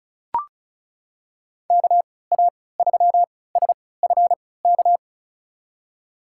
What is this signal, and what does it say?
Morse code, heard as a keyed tone, about 36 wpm, spaced as for 20 wpm, 700 Hz. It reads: KA3SFK
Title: USA Call Signs - 6 Characters - 3x Character Spacing - 35wpm